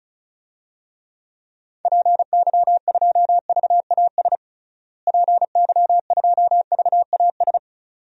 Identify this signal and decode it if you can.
Morse code, heard as a keyed tone, about 36 wpm, 700 Hz. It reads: PY2VAS PY2VAS
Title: DXCC-100 Call Signs - 6 Characters - Repeated 2x - 35wpm